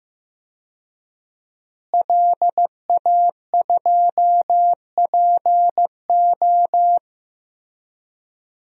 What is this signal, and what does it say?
Morse code, heard as a keyed tone, about 16 wpm, 700 Hz. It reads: LA2PO
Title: DXCC-100 Call Signs - 5 Characters - Repeated 3x - 15wpm